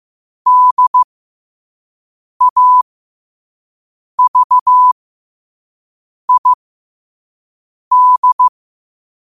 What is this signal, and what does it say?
Morse code, heard as a keyed tone, about 15 wpm, spaced as for 5 wpm, 1000 Hz. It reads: DAVID
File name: qotdfiles / 5wpm